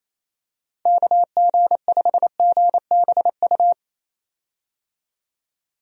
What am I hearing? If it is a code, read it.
Morse code, heard as a keyed tone, about 27 wpm, 700 Hz. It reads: KG5GBU
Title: USA Call Signs - 6 Characters - 28wpm